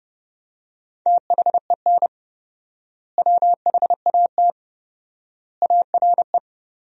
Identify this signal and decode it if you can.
Morse code, heard as a keyed tone, about 32 wpm, 700 Hz. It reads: THEN WHAT ARE
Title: 4x Word-Spacing - Sentences from Top 100 Words - 30wpm